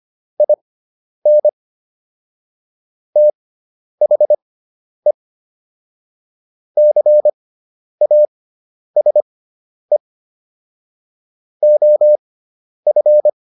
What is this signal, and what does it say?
Morse code, heard as a keyed tone, about 25 wpm, spaced as for 10 wpm, 600 Hz. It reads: IN THE CASE OF
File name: WD304 10